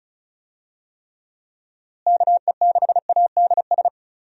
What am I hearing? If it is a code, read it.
Morse code, heard as a keyed tone, about 36 wpm, 700 Hz. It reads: KE6ADS
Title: USA Call Signs - 6 Characters - Repeated 2x - 35wpm